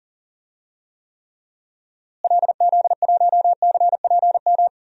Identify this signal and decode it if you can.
Morse code, heard as a keyed tone, about 40 wpm, 700 Hz. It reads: LZ1CPM
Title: DXCC-100 Call Signs - 6 Characters - Repeated 3x - 40wpm